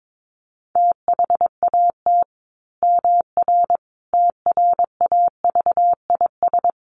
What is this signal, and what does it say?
Morse code, heard as a keyed tone, about 22 wpm, 700 Hz. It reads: THAT MR TRAVIS
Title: A Sound Of Thunder 22wpm